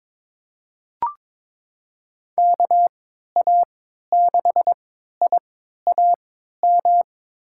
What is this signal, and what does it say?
Morse code, heard as a keyed tone, about 22 wpm, spaced as for 13 wpm, 700 Hz. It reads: KA6IAM
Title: USA Call Signs - 6 Characters - 3x Character Spacing - 22wpm